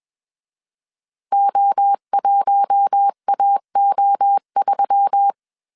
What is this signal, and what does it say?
Morse code, heard as a keyed tone, about 21 wpm, 800 Hz. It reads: O1AO3
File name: CW / DET / 1 6 1500C S B 20wpm 2048